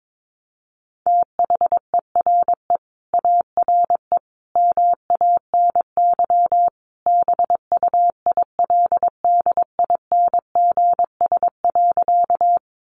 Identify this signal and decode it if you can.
Morse code, heard as a keyed tone, about 22 wpm, 700 Hz. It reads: THERE ARE MANY BUILDINGS.
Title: Easy Phrases 22wpm